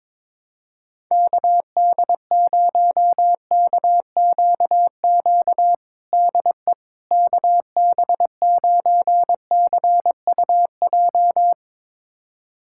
Morse code, heard as a keyed tone, about 22 wpm, 700 Hz. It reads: KD0KQQ DE KB9CUJ
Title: USA Callsign DE Callsign - 6 Characters - 22wpm